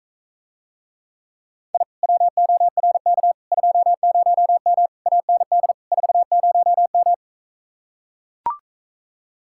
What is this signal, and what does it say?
Morse code, heard as a keyed tone, about 44 wpm, 700 Hz. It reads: I WORK 20M AND 40M